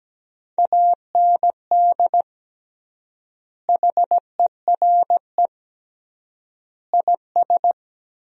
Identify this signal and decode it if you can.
Morse code, heard as a keyed tone, about 17 wpm, 700 Hz. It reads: AND HERE IS
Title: 3x Word-Spacing - Sentences from Top 500 Words - 17wpm